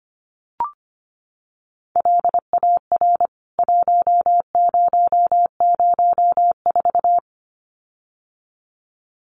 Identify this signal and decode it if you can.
Morse code, heard as a keyed tone, about 25 wpm, 700 Hz. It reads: LAR 1004
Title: CWT Exchange 25wpm